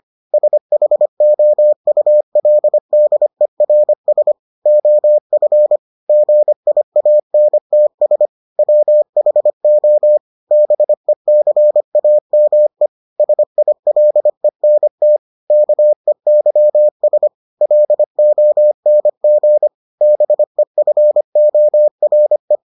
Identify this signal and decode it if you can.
Morse code, heard as a keyed tone, about 25 wpm, 600 Hz. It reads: SHOULDERS OF GIANTS WHO BECAME SILENT KEYS LONG BEFORE